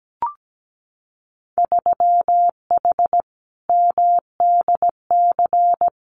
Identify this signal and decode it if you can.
Morse code, heard as a keyed tone, about 17 wpm, 700 Hz. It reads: 3H MDC
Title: Winter Field Day Exchanges Encore 17wpm